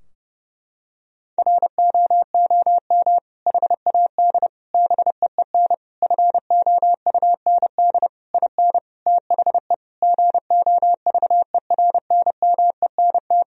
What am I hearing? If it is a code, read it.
Morse code, heard as a keyed tone, about 30 wpm, 700 Hz. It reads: ROOM HAD BEEN FOUND IN THE GOVERNMENT
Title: Sentences from Top 1000 Words - Review Part 3 - 30wpm